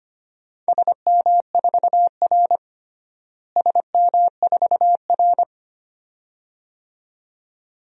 Morse code, heard as a keyed tone, about 25 wpm, 700 Hz. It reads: SM4R SM4R
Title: DXCC-100 Call Signs - 4 Characters - Repeated 2x - 25wpm